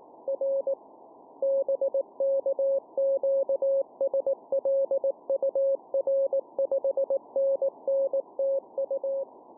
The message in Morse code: R BK QSL UR 5NN TU